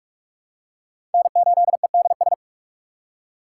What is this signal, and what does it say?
Morse code, heard as a keyed tone, about 42 wpm, 700 Hz. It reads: N8EDS
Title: USA Call Signs - 5 Characters - 4x Character Spacing - 45wpm